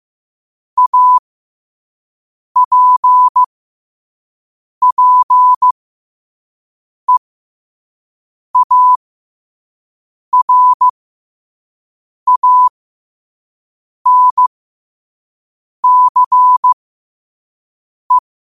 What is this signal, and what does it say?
Morse code, heard as a keyed tone, about 15 wpm, spaced as for 5 wpm, 1000 Hz. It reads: APPEARANCE